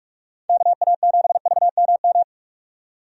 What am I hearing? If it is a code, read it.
Morse code, heard as a keyed tone, about 46 wpm, 700 Hz. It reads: KA7VMM
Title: USA Call Signs - 6 Characters - 45wpm